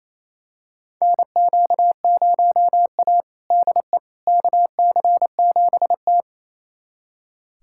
Morse code, heard as a keyed tone, about 28 wpm, 700 Hz. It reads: NQ0A DE KC7T
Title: USA Callsign DE Callsign - 4 Characters - 28wpm